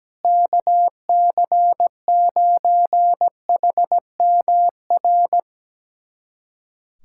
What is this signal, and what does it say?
Morse code, heard as a keyed tone, about 17 wpm, 700 Hz. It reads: KC9HMR
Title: USA Callsign DE Callsign 17wpm